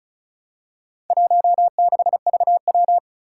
Morse code, heard as a keyed tone, about 35 wpm, 700 Hz. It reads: 16VW
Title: Four Challenging Letter-Number Combinations 35wpm